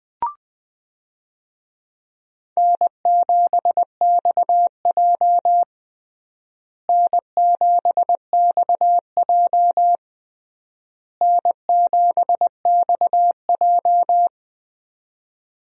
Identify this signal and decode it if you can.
Morse code, heard as a keyed tone, about 19 wpm, 700 Hz. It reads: N7XJ N7XJ N7XJ
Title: USA Call Signs - 4 Characters - Repeated 3x - 20wpm